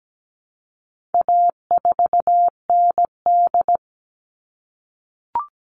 Morse code, heard as a keyed tone, about 17 wpm, 700 Hz. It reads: A4ND
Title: All Cut Numbers - 4-Digit 17wpm